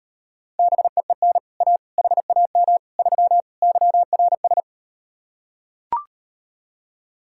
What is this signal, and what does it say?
Morse code, heard as a keyed tone, about 39 wpm, 700 Hz. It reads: BEEN A HAM 3 YRS